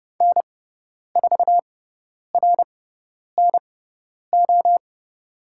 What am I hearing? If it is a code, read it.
Morse code, heard as a keyed tone, about 30 wpm, spaced as for 10 wpm, 700 Hz. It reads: N4RNO
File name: Lesson 17g - Letter u - Call Signs 30wpm